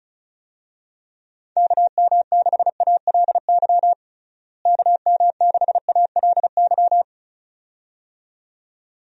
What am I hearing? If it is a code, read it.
Morse code, heard as a keyed tone, about 36 wpm, 700 Hz. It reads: KM6ALY KM6ALY
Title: USA Call Signs - 6 Characters - Repeated 2x - 35wpm